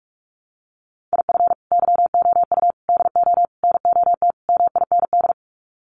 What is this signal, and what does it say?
Morse code, heard as a keyed tone, about 44 wpm, 700 Hz. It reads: IF YOU DO NOT MIND
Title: Sets of 5 Words 45wpm